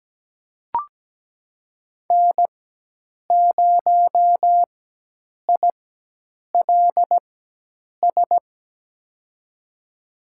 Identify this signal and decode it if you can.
Morse code, heard as a keyed tone, about 17 wpm, spaced as for 8 wpm, 700 Hz. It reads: N0ILS
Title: USA Call Signs - 5 Characters - 4x Character Spacing - 17wpm